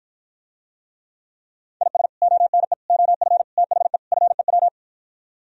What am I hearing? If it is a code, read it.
Morse code, heard as a keyed tone, about 52 wpm, 700 Hz. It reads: IS ONE OF THE FEW